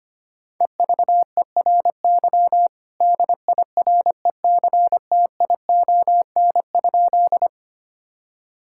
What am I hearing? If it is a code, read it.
Morse code, heard as a keyed tone, about 25 wpm, 700 Hz. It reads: EVERY DIRECTION?
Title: Sentences from Top 1000 Words - Review Part 5 - 25wpm